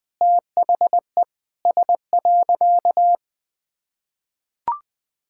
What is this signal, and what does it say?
Morse code, heard as a keyed tone, about 20 wpm, 700 Hz. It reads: THE S.
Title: Sentences with 6-Word Phrases - 20wpm